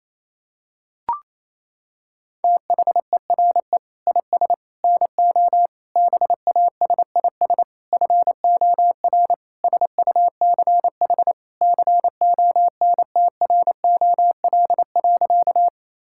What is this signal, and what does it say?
Morse code, heard as a keyed tone, about 27 wpm, 700 Hz. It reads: THERE IS NO BASIS FOR SUCH CONTROL.
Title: Sentences from Top 1000 Words - Review Part 5 - 28wpm